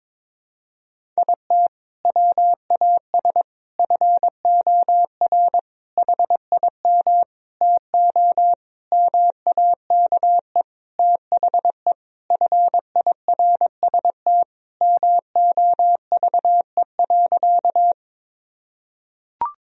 Morse code, heard as a keyed tone, about 22 wpm, 700 Hz. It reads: IT WAS FOR HIM TO MAKE THE FIRST MOVE.
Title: Sentences from Top 1000 Words - Review Part 1 - 22wpm